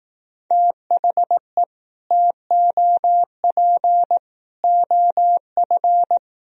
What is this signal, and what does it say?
Morse code, heard as a keyed tone, about 18 wpm, 700 Hz. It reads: THE TOP OF
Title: Speed-Racing - Sets of 3 Words 15wpm